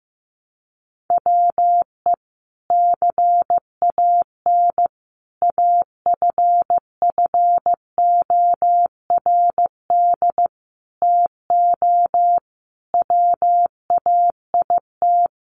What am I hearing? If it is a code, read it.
Morse code, heard as a keyed tone, about 15 wpm, 700 Hz. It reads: WE CAN AFFORD TO WAIT
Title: Sentences from Top 1100 Words 15wpm